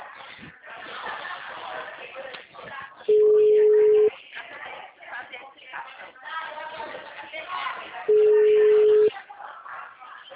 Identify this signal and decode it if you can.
Ringback tone (Europe, 425 Hz), with speech.